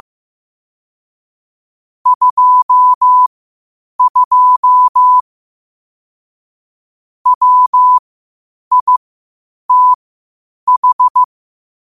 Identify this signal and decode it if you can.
Morse code, heard as a keyed tone, about 15 wpm, spaced as for 8 wpm, 1000 Hz. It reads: 22 WITH
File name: qotdfiles / 8wpm